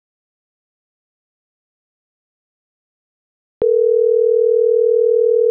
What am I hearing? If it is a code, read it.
Ringback tone (North America), 440 + 480 Hz.